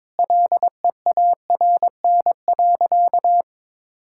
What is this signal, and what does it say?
Morse code, heard as a keyed tone, about 22 wpm, 700 Hz. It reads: LEARN.
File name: practice / 22wpm / Sentences with 2-Word Phrases - 22wpm